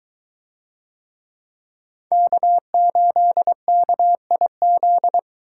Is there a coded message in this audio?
Yes: Morse code from 2.1 to 5.2 s, heard as a keyed tone, about 22 wpm, 700 Hz. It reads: K8KIZ